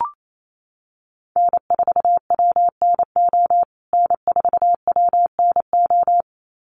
Morse code, heard as a keyed tone, about 28 wpm, 700 Hz. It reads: N4WNO N4WNO
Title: USA Call Signs Repeated 2x 28wpm